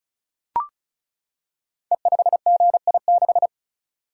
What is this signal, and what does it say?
Morse code, heard as a keyed tone, about 36 wpm, 700 Hz. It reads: E5GI6
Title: Code Groups - 5 Letter:Numbers - 35wpm